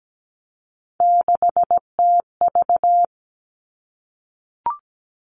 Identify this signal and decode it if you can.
Morse code, heard as a keyed tone, about 17 wpm, 700 Hz. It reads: 6TV
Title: Three Challenging Letter-Number Combinations 17wpm